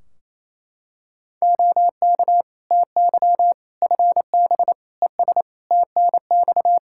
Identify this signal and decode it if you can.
Morse code, heard as a keyed tone, about 28 wpm, 700 Hz. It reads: OK TY FB ES TNX
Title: K1USN SST QSO Practice 28wpm